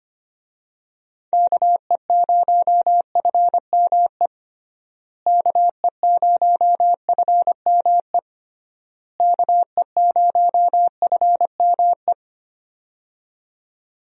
Morse code, heard as a keyed tone, about 25 wpm, 700 Hz. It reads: KE0FME KE0FME KE0FME
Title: USA Call Signs - 6 Characters - Repeated 3x - 25wpm